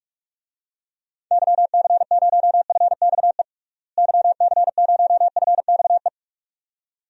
Morse code, heard as a keyed tone, about 46 wpm, 700 Hz. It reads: YC0FXE YC0FXE
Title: DXCC-100 Call Signs - 6 Characters - Repeated 2x - 45wpm